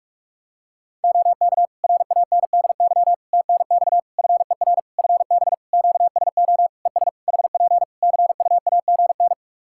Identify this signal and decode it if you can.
Morse code, heard as a keyed tone, about 46 wpm, 700 Hz. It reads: OK RANDY TNX FER FB QSO ES HP CUAGN
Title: Practice QSO - Exchange 4 - Ending - 45wpm